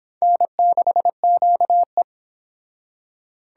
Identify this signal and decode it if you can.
Morse code, heard as a keyed tone, about 25 wpm, 700 Hz. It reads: N6QE